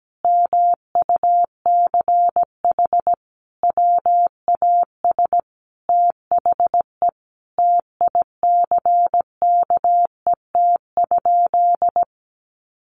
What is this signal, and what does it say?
Morse code, heard as a keyed tone, about 17 wpm, 700 Hz. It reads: MUCH WAS THE TICKET?
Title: Sentences from Top 1400 Words 17wpm